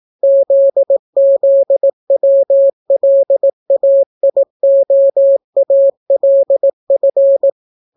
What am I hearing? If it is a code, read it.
Morse code, heard as a keyed tone, about 18 wpm, 550 Hz. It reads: ZZWLAIOALF